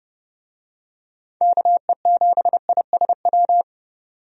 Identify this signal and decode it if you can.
Morse code, heard as a keyed tone, about 30 wpm, 700 Hz. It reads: KE7ISW
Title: USA Call Signs - 6 Characters - 30wpm